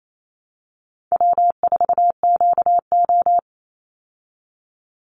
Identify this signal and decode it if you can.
Morse code, heard as a keyed tone, about 28 wpm, 700 Hz. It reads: W4QO
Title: Easy Call Signs Repeated 3x - 28wpm